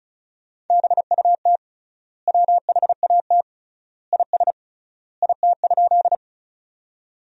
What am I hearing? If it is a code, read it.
Morse code, heard as a keyed tone, about 36 wpm, 700 Hz. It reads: BUT WHAT IS IT?
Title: 3x Word-Spacing - Sentences from Top 100 Words - 35wpm